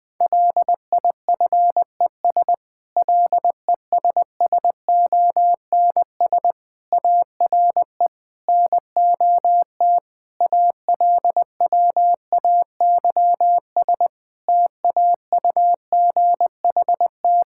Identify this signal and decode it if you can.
Morse code, heard as a keyed tone, about 20 wpm, 700 Hz. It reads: LIFES LESSONS ARE NOT ALWAYS TAUGHT
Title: Difficult Sentences with Hints - 20wpm